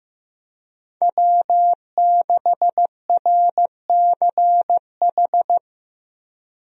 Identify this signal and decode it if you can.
Morse code, heard as a keyed tone, about 15 wpm, 700 Hz. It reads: W6RCH
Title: USA Call Signs - 5 Characters - Repeated 2x - 15wpm